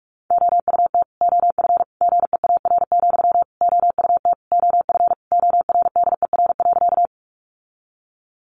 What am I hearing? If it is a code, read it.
Morse code, heard as a keyed tone, about 44 wpm, 700 Hz. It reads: OUT OF GEAR, OUT OF ORDER.
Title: Sentences from Top 1400 Words 45wpm